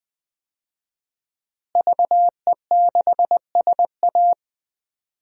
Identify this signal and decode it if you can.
Morse code, heard as a keyed tone, about 20 wpm, 700 Hz. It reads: VE6SA
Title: DXCC-100 Call Signs - 5 Characters - Repeated 3x - 20wpm